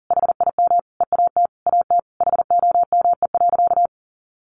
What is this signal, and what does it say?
Morse code, heard as a keyed tone, about 40 wpm, 700 Hz. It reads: HIM EAT AT HOME.
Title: Sentences from Top 1400 Words 40wpm